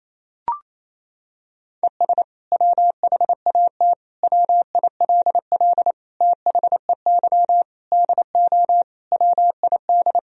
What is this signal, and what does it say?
Morse code, heard as a keyed tone, about 28 wpm, 700 Hz. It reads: ES WHAT WILL THEY DO WID
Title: Sentences from Top 200 Words and Abbreviations 28wpm